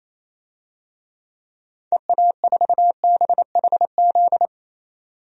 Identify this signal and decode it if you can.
Morse code, heard as a keyed tone, about 28 wpm, 700 Hz. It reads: EA4BHZ
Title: DXCC-100 Call Signs - 6 Characters - 28wpm